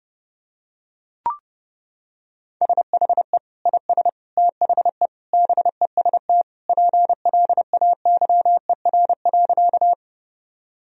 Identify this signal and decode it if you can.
Morse code, heard as a keyed tone, about 30 wpm, 700 Hz. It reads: SHE IS THE BEST PLAYER.
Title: Sentences from Top 1000 Words - Review Part 4 - 30wpm